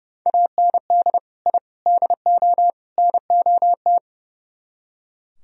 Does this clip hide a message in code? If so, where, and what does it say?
Yes: Morse code from 0.3 to 4.0 s, heard as a keyed tone, about 39 wpm, 700 Hz. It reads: AND I DO NOT